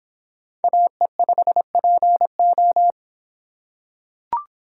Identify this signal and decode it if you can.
Morse code, heard as a keyed tone, about 24 wpm, 700 Hz. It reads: AE5PO